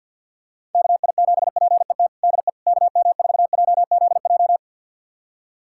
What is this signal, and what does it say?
Morse code, heard as a keyed tone, about 51 wpm, 700 Hz. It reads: KI7PET DE KM4JZJ
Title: USA Callsign DE Callsign - 6 Characters - 50wpm